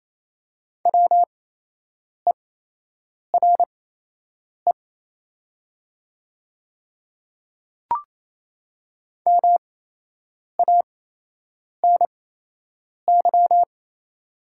Morse code, heard as a keyed tone, about 28 wpm, spaced as for 8 wpm, 700 Hz. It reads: WERE MANY
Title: 8x Character Spacing - Top 200 Words - 28wpm